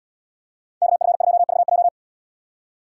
Morse code, heard as a keyed tone, about 77 wpm, 700 Hz. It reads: KF2VJ